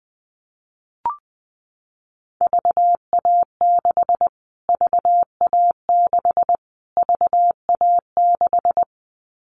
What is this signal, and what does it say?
Morse code, heard as a keyed tone, about 20 wpm, 700 Hz. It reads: VA6 VA6 VA6
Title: Callsign Prefixes for Top 25 DXCC - Repeated 3x - 20wpm